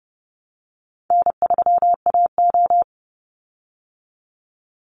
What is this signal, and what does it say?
Morse code, heard as a keyed tone, about 30 wpm, 700 Hz. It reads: N3AO
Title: Easy Call Signs Repeated 2x - 30wpm